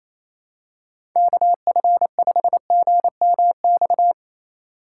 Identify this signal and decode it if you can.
Morse code, heard as a keyed tone, about 28 wpm, 700 Hz. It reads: KF5GMX
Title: USA Call Signs - 6 Characters - 2x Character Spacing - 28wpm